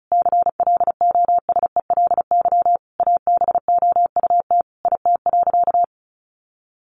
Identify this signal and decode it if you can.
Morse code, heard as a keyed tone, about 34 wpm, 700 Hz. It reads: CLOSELY ABOUT IT.
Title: Sentences from Top 1400 Words 35wpm